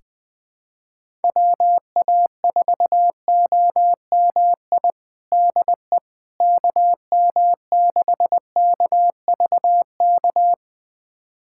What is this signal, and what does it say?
Morse code, heard as a keyed tone, about 20 wpm, 700 Hz. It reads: WA4OMI DE KM6KVK
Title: USA Callsign DE Callsign - 6 Characters - 20wpm